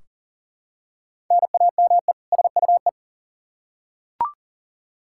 Morse code, heard as a keyed tone, about 40 wpm, 700 Hz. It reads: NAME SUE
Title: Practice QSO - Exchange 1 - Beginning - 40wpm